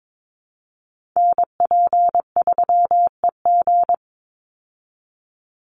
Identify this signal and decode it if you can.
Morse code, heard as a keyed tone, about 22 wpm, 700 Hz. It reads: NP3EG
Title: USA Call Signs 22wpm